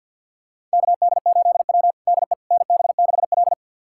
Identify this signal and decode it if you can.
Morse code, heard as a keyed tone, about 53 wpm, 700 Hz. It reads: KD8W DE NB6L
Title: USA Callsign DE Callsign - 4 Characters - 50wpm